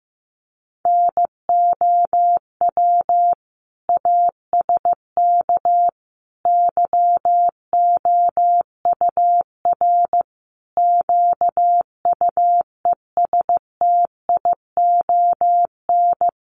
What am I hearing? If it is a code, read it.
Morse code, heard as a keyed tone, about 14 wpm, 700 Hz. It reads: NOW ASK YOUR QUESTION
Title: Sentences from Top 1100 Words 15wpm